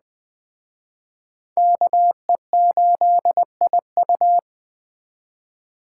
Morse code, heard as a keyed tone, about 20 wpm, 700 Hz. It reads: KE8IU